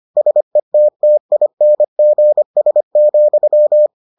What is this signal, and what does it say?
Morse code, heard as a keyed tone, about 25 wpm, 600 Hz. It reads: SETTINGS,